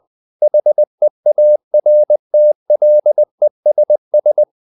Morse code, heard as a keyed tone, about 20 wpm, 600 Hz. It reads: HEARTLESS
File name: the old man and the sea / the old man and the sea part 1 20wpm